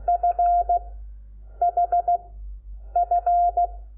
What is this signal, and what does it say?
Morse code, heard as a keyed tone, about 16 wpm, spaced as for 8 wpm, 700 Hz. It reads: FHF
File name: Lecciones / CLASE Nº5